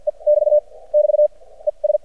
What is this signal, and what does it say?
Morse code, heard as a keyed tone, about 52 wpm, 600 Hz. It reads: E <BT> <BT> E S
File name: archived / CRSnet